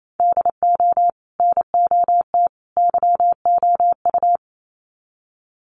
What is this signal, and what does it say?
Morse code, heard as a keyed tone, about 28 wpm, 700 Hz. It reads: DO NOT YOU
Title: Sets of 3 Words 28wpm